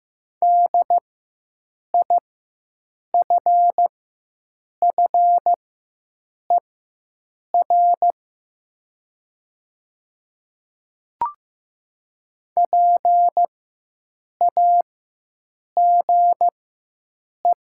Morse code, heard as a keyed tone, about 15 wpm, spaced as for 7 wpm, 700 Hz. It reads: DIFFER PAGE
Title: 4x Character Spacing - Top 300 Words - 15wpm